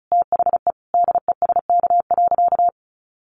Morse code, heard as a keyed tone, about 34 wpm, 700 Hz. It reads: THE DESK.
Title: A Sound Of Thunder 35wpm